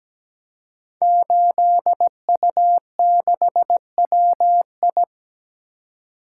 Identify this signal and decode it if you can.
Morse code, heard as a keyed tone, about 17 wpm, 700 Hz. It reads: 8U6WI
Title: Code Groups - 5 Letter:Numbers - 17wpm